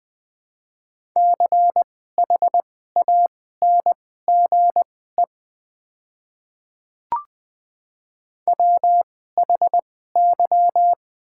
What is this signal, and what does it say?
Morse code, heard as a keyed tone, about 20 wpm, spaced as for 14 wpm, 700 Hz. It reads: CHANGE WHY
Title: 2x Character Spacing - Top 200 Words - 20wpm